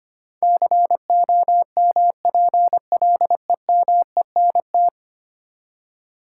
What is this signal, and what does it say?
Morse code, heard as a keyed tone, about 25 wpm, 690 Hz. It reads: COMPLEMENT
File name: 10 Letter Words - First 3 Letters Given - 25wpm